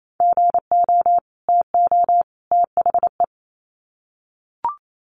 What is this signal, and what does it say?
Morse code, heard as a keyed tone, about 28 wpm, 700 Hz. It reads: GO TO THE
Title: Sets of 3 Words 28wpm